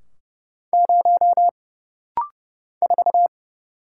Morse code, heard as a keyed tone, about 30 wpm, 700 Hz. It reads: 0 4
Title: Single Letter-Number - Rapid-Fire 30wpm